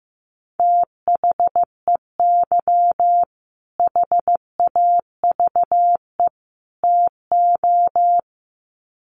Morse code, heard as a keyed tone, about 15 wpm, 700 Hz. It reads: THEY HAVE TO
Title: Sets of 3 Words 15wpm